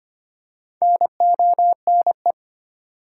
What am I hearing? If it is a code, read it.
Morse code, heard as a keyed tone, about 25 wpm, 700 Hz. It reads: NONE
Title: Four Letter Words - 25wpm